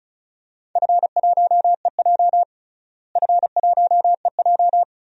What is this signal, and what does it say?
Morse code, heard as a keyed tone, about 36 wpm, 700 Hz. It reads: F1EJ F1EJ
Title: DXCC-100 Call Signs - 4 Characters - Repeated 2x - 35wpm